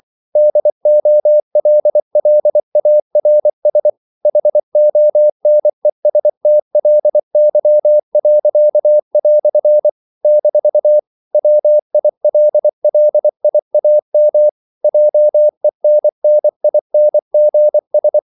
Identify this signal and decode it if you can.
Morse code, heard as a keyed tone, about 24 wpm, 600 Hz. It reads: DOLLARS HONESTLY." - WILLIAM JENNINGS
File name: CW / Practice / EnglishQuotations0324